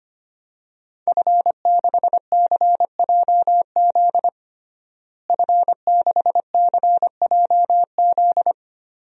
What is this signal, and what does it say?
Morse code, heard as a keyed tone, about 25 wpm, 700 Hz. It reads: F6CJZ F6CJZ
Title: DXCC-100 Call Signs - 5 Characters - Repeated 3x - 25wpm